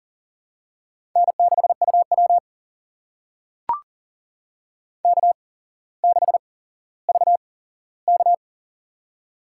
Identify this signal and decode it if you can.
Morse code, heard as a keyed tone, about 40 wpm, spaced as for 17 wpm, 700 Hz. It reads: N6UW K6VK
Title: USA Call Signs - 4 Characters - 8x Character Spacing - 40wpm